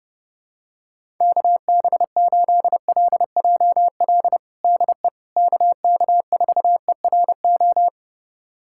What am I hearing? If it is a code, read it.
Morse code, heard as a keyed tone, about 30 wpm, 700 Hz. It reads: KB8LJL DE KK4ERO
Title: USA Callsign DE Callsign 30wpm